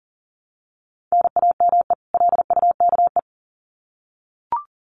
Morse code, heard as a keyed tone, about 40 wpm, 700 Hz. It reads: NAME LUKE
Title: Common QSO Phrases 40wpm